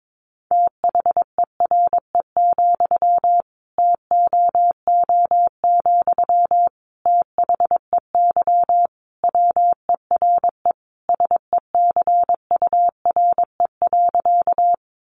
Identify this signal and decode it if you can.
Morse code, heard as a keyed tone, about 22 wpm, 700 Hz. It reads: THERE, TOO, THEY WERE SECURE.
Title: Sentences from Top 1200 Words 22wpm